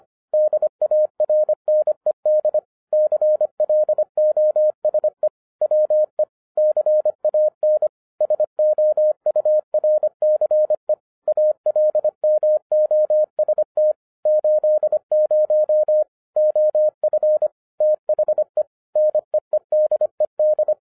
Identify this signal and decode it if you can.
Morse code, heard as a keyed tone, about 25 wpm, 610 Hz. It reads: DARNED CLOSE WE CAN SOURCE ALMOST 80 OF THE NEEDED
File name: 2012-01-mp3 / n14-FUN WITH NUMBERS-25wpm